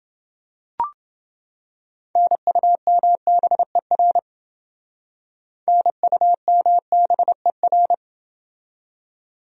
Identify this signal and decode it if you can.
Morse code, heard as a keyed tone, about 29 wpm, 700 Hz. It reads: NUMBER NUMBER